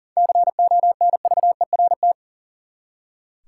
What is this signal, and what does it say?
Morse code, heard as a keyed tone, about 40 wpm, 700 Hz. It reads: CONVERT
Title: Seven Letter Words - 40wpm